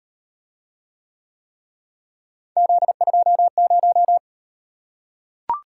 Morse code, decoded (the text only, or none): Z20